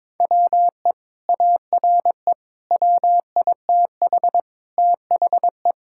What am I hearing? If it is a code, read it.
Morse code, heard as a keyed tone, about 22 wpm, 700 Hz. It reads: WE ARE WITH THE
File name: practice / 22wpm / Sentences with 2-Word Phrases - 22wpm